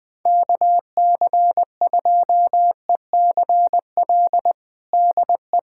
Morse code, heard as a keyed tone, about 20 wpm, 700 Hz. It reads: KC2ECL DE
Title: USA Callsign DE Callsign - 6 Characters - 20wpm